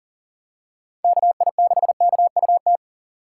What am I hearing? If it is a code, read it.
Morse code, heard as a keyed tone, about 40 wpm, 700 Hz. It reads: KI6KUT
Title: USA Call Signs - 6x Character Spacing - 40wpm